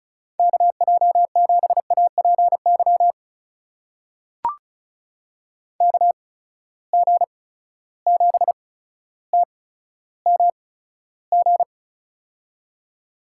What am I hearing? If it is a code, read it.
Morse code, heard as a keyed tone, about 36 wpm, spaced as for 15 wpm, 700 Hz. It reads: KJ7APY KG7TMG
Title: USA Call Signs - 8x Character Spacing - 35wpm